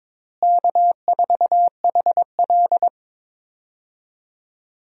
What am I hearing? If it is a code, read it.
Morse code, heard as a keyed tone, about 22 wpm, 700 Hz. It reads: K4HL
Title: USA Call Signs - 4 Characters - Repeated 3x - 22wpm